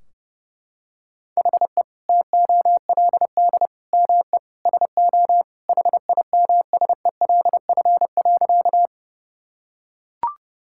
Morse code, heard as a keyed tone, about 32 wpm, 700 Hz. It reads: HE TOLD ME SO HIMSELF.
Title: Sentences from Top 1600 Words 30wpm